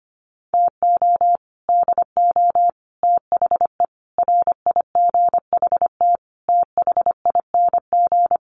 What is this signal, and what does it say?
Morse code, heard as a keyed tone, about 25 wpm, 700 Hz. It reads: TO DO THE RIGHT THING
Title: Sets of 5 Words 25wpm